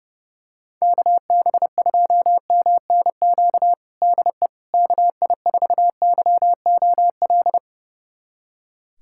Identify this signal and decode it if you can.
Morse code, heard as a keyed tone, about 32 wpm, 700 Hz. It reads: KB2MNQ DE KI4YOL
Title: USA Callsign DE Callsign - 6 Characters - 30wpm